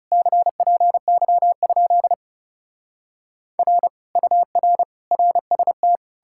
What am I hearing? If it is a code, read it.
Morse code, heard as a keyed tone, about 40 wpm, 700 Hz. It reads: CPY? R UR RST